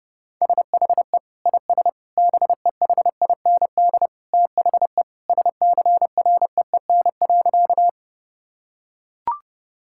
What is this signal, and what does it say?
Morse code, heard as a keyed tone, about 30 wpm, 700 Hz. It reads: SHE IS BEHIND THE SCREEN.
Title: Sentences from Top 1000 Words - Review Part 6 - 30wpm